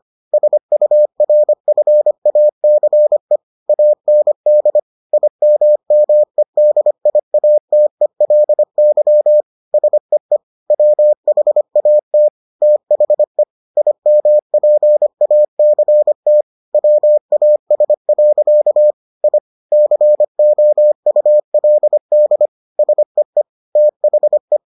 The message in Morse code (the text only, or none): SURFACE AND IMMEDIATELY SEE WHAT THE IMPACT WAS. I COULD SEE THE